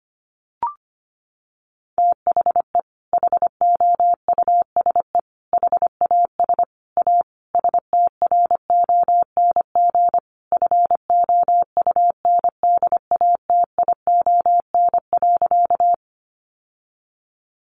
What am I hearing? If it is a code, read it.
Morse code, heard as a keyed tone, about 25 wpm, 700 Hz. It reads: THE HOUSE HAS A STRONG FOUNDATION.
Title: Sentences from Top 1200 Words 25wpm